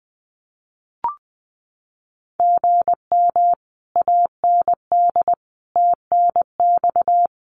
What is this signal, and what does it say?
Morse code, heard as a keyed tone, about 20 wpm, 700 Hz. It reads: GM AND TNX
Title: Common QSO Phrases Encore 20wpm